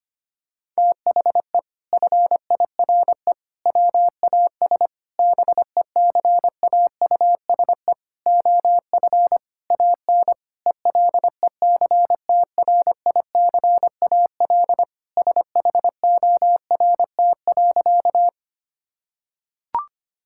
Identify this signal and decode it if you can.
Morse code, heard as a keyed tone, about 25 wpm, 700 Hz. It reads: THE FIRE WAS BECAUSE OF AN ELECTRICAL SHORT.
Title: Sentences from Top 1000 Words - Review Part 6 - 25wpm